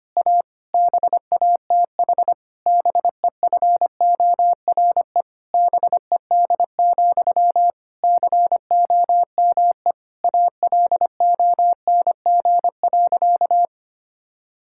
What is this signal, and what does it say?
Morse code, heard as a keyed tone, about 25 wpm, 700 Hz. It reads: A BATH BEFORE BED, COME ALONG.
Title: Sentences from Top 1600 Words 25wpm